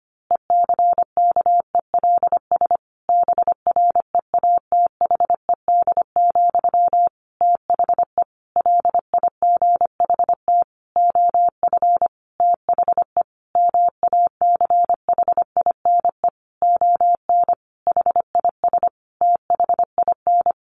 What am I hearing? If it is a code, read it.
Morse code, heard as a keyed tone, about 25 wpm, 700 Hz. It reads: ECKELS BREATHED, THE LIGHT OF THE MACHINE ON HIS THIN
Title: A Sound Of Thunder 25wpm